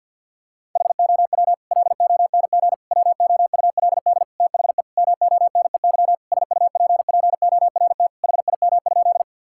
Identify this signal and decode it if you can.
Morse code, heard as a keyed tone, about 50 wpm, 700 Hz. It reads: HOW LONG WOULD THE MONEY SUPPORT HIM?
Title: Sentences from Top 1000 Words - Review Part 5 - 50wpm